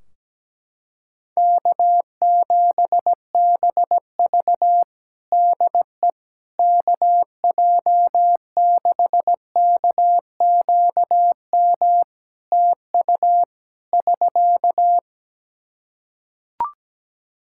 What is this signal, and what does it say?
Morse code, heard as a keyed tone, about 17 wpm, 700 Hz. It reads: K7BV DE KJ6KQM TU <SK>
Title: Practice QSO - Exchange 4 - Ending - 17wpm